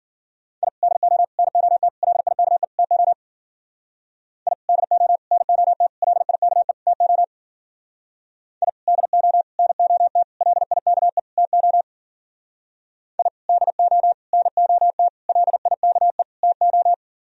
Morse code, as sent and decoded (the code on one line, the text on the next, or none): .. / -.. --- / -. --- - / .-.. .. -.- . / - --- / .. / -.. --- / -. --- - / .-.. .. -.- . / - --- / .. / -.. --- / -. --- - / .-.. .. -.- . / - --- / .. / -.. --- / -. --- - / .-.. .. -.- . / - ---
I DO NOT LIKE TO I DO NOT LIKE TO I DO NOT LIKE TO I DO NOT LIKE TO